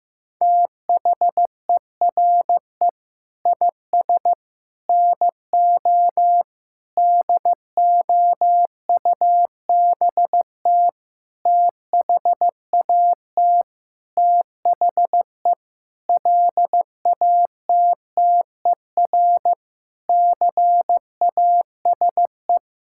Morse code, heard as a keyed tone, about 15 wpm, 700 Hz. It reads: THERE IS NO DOUBT THAT THE LATTER CASE
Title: Sentences from Top 1600 Words 15wpm